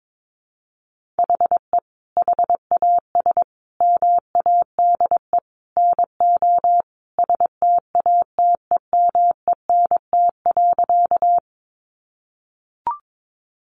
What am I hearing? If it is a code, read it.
Morse code, heard as a keyed tone, about 22 wpm, 700 Hz. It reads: HE HAS MADE NO STATEMENT.
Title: Sentences from Top 1200 Words 22wpm